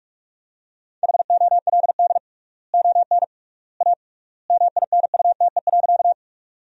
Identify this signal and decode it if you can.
Morse code, heard as a keyed tone, about 46 wpm, 700 Hz. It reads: HOLD ON A MINUTE.
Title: 3x Word-Spacing - Sentences from Top 500 Words - 45wpm